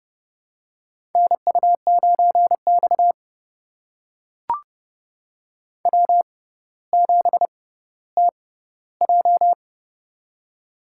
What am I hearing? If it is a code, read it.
Morse code, heard as a keyed tone, about 30 wpm, spaced as for 15 wpm, 700 Hz. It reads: NU9X W7TJ
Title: USA Call Signs - 4 Characters - 6x Character Spacing - 30wpm